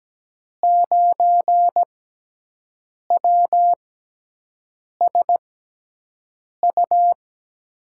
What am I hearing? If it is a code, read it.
Morse code, heard as a keyed tone, about 17 wpm, spaced as for 6 wpm, 700 Hz. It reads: 9WSU
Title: USA Call Signs - 5 Characters - 6x Character Spacing - 17wpm